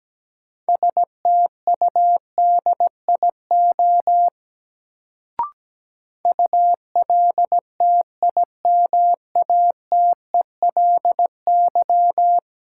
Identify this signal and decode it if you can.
Morse code, heard as a keyed tone, about 17 wpm, 700 Hz. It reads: STUDIO ULTIMATELY
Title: Sentences from Top 1200 Words - New Word Drill - 17wpm